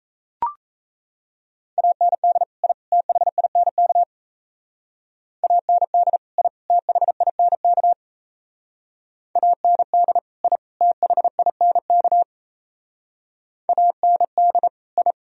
Morse code, heard as a keyed tone, about 34 wpm, 700 Hz. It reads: AND I THINK AND I THINK AND I THINK AND I
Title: Speed-Racing - Sets of 3 Words 28wpm